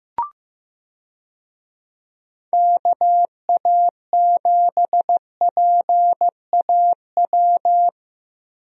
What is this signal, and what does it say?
Morse code, heard as a keyed tone, about 15 wpm, 700 Hz. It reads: KA7PAW
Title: USA Call Signs - 6 Characters - Repeated 3x - 15wpm